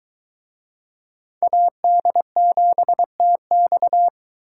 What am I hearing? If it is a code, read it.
Morse code, heard as a keyed tone, about 23 wpm, 700 Hz. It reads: AD7TX